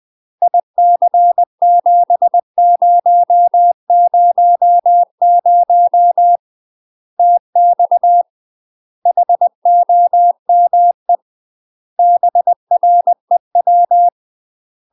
Morse code, heard as a keyed tone, about 20 wpm, 700 Hz. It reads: IC7000 TX HOME BREW
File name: UK MASTER QSOs TRAINING / 20WPM / UK MASTER QSOs 0052